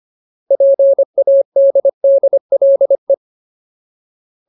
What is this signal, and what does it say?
Morse code, heard as a keyed tone, about 25 wpm, 550 Hz. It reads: PADDLE
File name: Ultimate Word List / 25WPM Ultimate